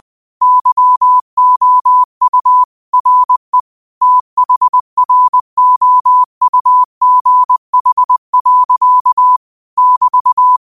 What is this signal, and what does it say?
Morse code, heard as a keyed tone, about 20 wpm, 1000 Hz. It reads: YOU RE THROUGH. <BT>
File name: qotdfiles / 20wpm